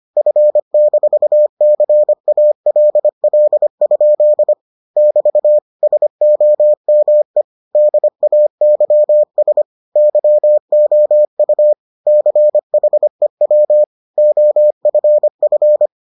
Morse code, heard as a keyed tone, about 25 wpm, 600 Hz. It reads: F-CALL? <BT> SOME DAYS YOU CHEW OFF